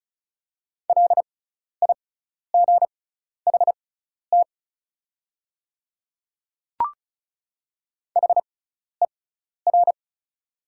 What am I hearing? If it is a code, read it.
Morse code, heard as a keyed tone, about 36 wpm, spaced as for 12 wpm, 690 Hz. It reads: LIGHT HER
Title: 6x Character Spacing - Top 200 Words - 35wpm